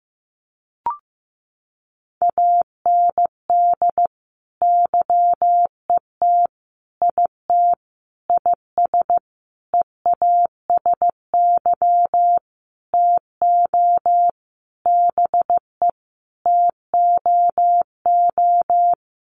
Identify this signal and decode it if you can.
Morse code, heard as a keyed tone, about 14 wpm, 700 Hz. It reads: AND YET IT IS EASY TO BE TOO
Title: Sentences from Top 1400 Words 15wpm